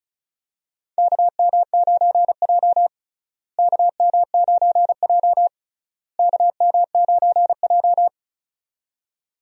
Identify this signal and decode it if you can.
Morse code, heard as a keyed tone, about 36 wpm, 700 Hz. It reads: KM9J KM9J KM9J
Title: USA Call Signs - 4 Characters - Repeated 3x - 35wpm